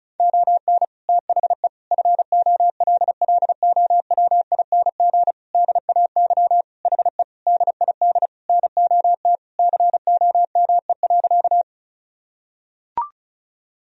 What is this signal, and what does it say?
Morse code, heard as a keyed tone, about 36 wpm, 700 Hz. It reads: ON THE FOLLOWING DAY HE DID NOT COME.
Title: Sentences from Top 1000 Words - Review Part 4 - 35wpm